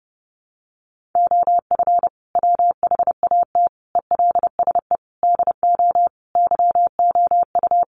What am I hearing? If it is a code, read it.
Morse code, heard as a keyed tone, about 30 wpm, 700 Hz. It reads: OF WHAT ELSE DO YOU
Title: Sentences from Top 1400 Words 30wpm